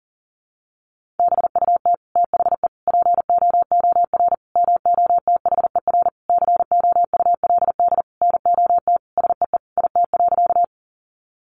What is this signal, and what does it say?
Morse code, heard as a keyed tone, about 40 wpm, 700 Hz. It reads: BUT THE POOR MOTHER COULD NOT SEE IT.
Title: Sentences from Top 1000 Words 40wpm